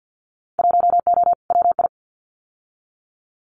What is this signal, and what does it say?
Morse code, heard as a keyed tone, about 50 wpm, 700 Hz. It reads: 1O WI
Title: Winter Field Day Exchanges 50wpm